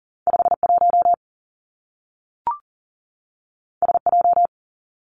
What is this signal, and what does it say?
Morse code, heard as a keyed tone, about 40 wpm, 700 Hz. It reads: 5 1 S J
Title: Two Challenging Letter-Number Combinations 40wpm